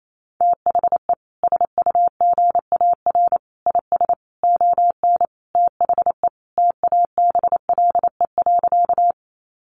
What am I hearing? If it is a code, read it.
Morse code, heard as a keyed tone, about 28 wpm, 700 Hz. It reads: THE SUGAR IS ON THE TABLE.
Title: Sentences from Top 1200 Words 28wpm